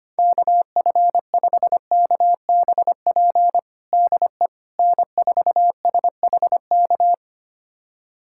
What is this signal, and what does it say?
Morse code, heard as a keyed tone, about 25 wpm, 700 Hz. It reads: KF5KBP DE N4SHK
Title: USA Callsign DE Callsign 25wpm